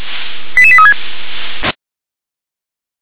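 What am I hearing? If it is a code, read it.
Five-tone selective call (ZVEI), heard as quick runs of five tones, tones of about 70 ms. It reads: 88836
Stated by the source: ZVEI five-tone selective call received on an SDR.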